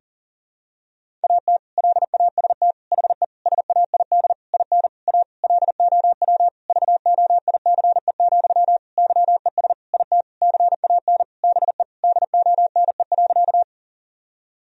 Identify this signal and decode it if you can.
Morse code, heard as a keyed tone, about 40 wpm, 700 Hz. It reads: AT LAST HE SAID IN A LOW VOICE, YES IT CAN BE DONE.
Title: Sentences from Top 1000 Words - Review Part 1 - 40wpm